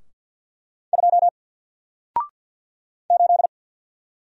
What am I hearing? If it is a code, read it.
Morse code, heard as a keyed tone, about 48 wpm, 700 Hz. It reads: 2 8
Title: Single Character (Letter:Number:Punctuation:Prosign) Alt 50wpm